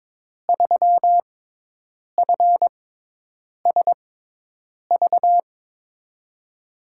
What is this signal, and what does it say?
Morse code, heard as a keyed tone, about 22 wpm, spaced as for 8 wpm, 700 Hz. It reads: 3FSV
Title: USA Call Signs - 6x Character Spacing - 22wpm